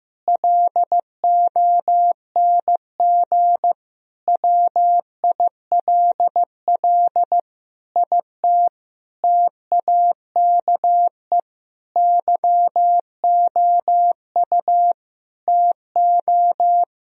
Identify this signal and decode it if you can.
Morse code, heard as a keyed tone, about 15 wpm, 700 Hz. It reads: LONG WILL IT TAKE YOU TO
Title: Sentences from Top 1000 Words - Review Part 2 - 15wpm